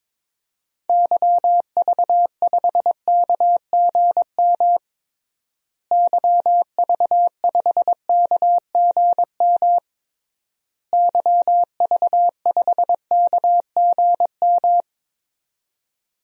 Morse code, heard as a keyed tone, about 22 wpm, 700 Hz. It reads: YV5KGM YV5KGM YV5KGM
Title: DXCC-100 Call Signs - 6 Characters - Repeated 3x - 22wpm